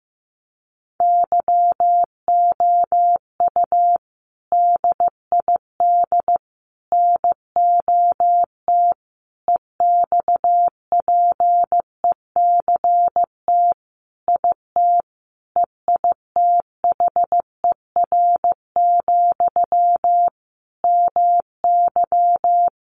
Morse code, heard as a keyed tone, about 15 wpm, 700 Hz. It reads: YOU DID NOT EXPECT IT EITHER, MY
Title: Sentences from Top 1100 Words 15wpm